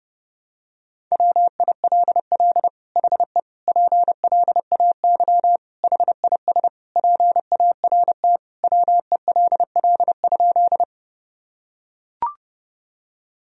Morse code, heard as a keyed tone, about 28 wpm, 700 Hz. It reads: WILL HE PLAY HIS PART WELL?
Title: Sentences from Top 1000 Words - Review Part 1 - 30wpm